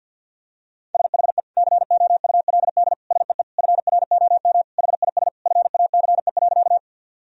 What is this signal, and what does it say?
Morse code, heard as a keyed tone, about 50 wpm, 700 Hz. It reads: SHE COULD SEE FROM HIS FACE.
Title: Sentences from Top 1000 Words - Review Part 2 - 50wpm